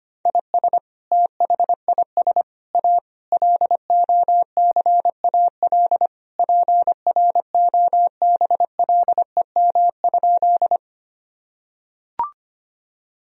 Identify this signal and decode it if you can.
Morse code, heard as a keyed tone, about 25 wpm, 700 Hz. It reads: IS THIS A LOCAL PROBLEM?
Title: Sentences from Top 1000 Words - Review Part 3 - 25wpm